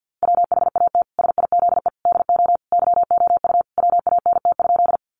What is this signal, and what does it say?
Morse code, heard as a keyed tone, about 50 wpm, 700 Hz. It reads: WHAT SIZE DO YOU WANT?
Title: Sentences from Top 400 Words 50wpm